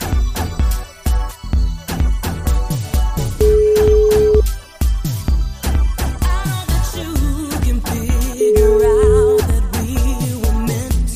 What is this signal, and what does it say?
Ringback tone (Europe, 425 Hz).